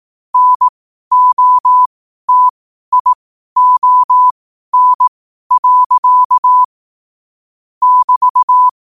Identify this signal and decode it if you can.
Morse code, heard as a keyed tone, about 18 wpm, spaced as for 12 wpm, 1000 Hz. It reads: NOTION. <BT>
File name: qotdfiles / 12wpm